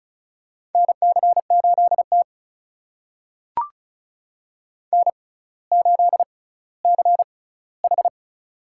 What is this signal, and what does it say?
Morse code, heard as a keyed tone, about 36 wpm, spaced as for 18 wpm, 700 Hz. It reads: NC8T N8CH
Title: USA Call Signs - 4 Characters - 6x Character Spacing - 35wpm